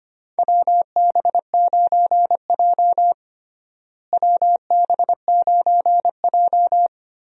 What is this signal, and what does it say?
Morse code, heard as a keyed tone, about 25 wpm, 700 Hz. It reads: WB9J WB9J
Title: USA Call Signs - 4 Characters - Repeated 3x - 25wpm